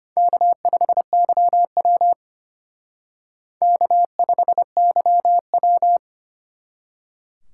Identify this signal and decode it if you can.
Morse code, heard as a keyed tone, about 32 wpm, 700 Hz. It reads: K5YW K5YW